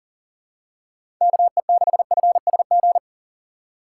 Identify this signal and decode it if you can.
Morse code, heard as a keyed tone, about 40 wpm, 700 Hz. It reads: KE6FSG
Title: USA Call Signs - 6 Characters - 6x Character Spacing - 40wpm